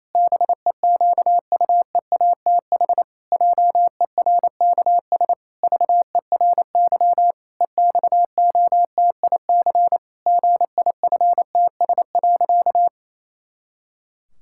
Morse code, heard as a keyed tone, about 28 wpm, 700 Hz. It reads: BEQUEATH JERKS VERY EXOTIC GIFTS.
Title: Pangram Challenge v2 - 28wpm